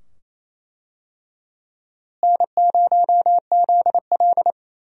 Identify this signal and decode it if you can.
Morse code, heard as a keyed tone, about 27 wpm, 700 Hz. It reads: N0ZL
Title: USA Call Signs - 4 Characters - Repeated 2x - 28wpm